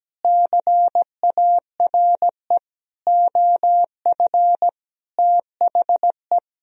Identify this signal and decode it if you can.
Morse code, heard as a keyed tone, about 17 wpm, 700 Hz. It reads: CARE OF THE
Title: Sentences from Top 400 Words and Abbreviations 17wpm